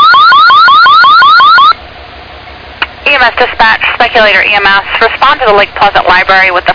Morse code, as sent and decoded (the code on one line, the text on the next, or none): none